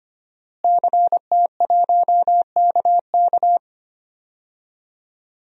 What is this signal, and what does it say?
Morse code, heard as a keyed tone, about 25 wpm, 700 Hz. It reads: CT1KK
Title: DXCC-100 Call Signs - 5 Characters - Repeated 3x - 25wpm